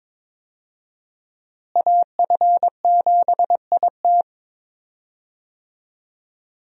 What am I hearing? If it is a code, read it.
Morse code, heard as a keyed tone, about 22 wpm, 700 Hz. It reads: AF7IT
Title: USA Call Signs - 5 Characters - Repeated 3x - 22wpm